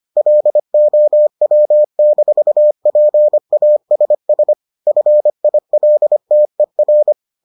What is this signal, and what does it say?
Morse code, heard as a keyed tone, about 25 wpm, 600 Hz. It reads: LOW-PASS FILTER